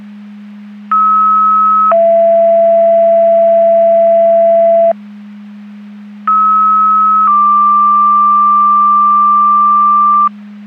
Two-tone page, heard as a one-second tone, then a three-second tone, about 1290 then 670 Hz, about 1290 then 1180 Hz. It is heard from 0.9 to 10.3 s.
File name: Portage 154.385 Paging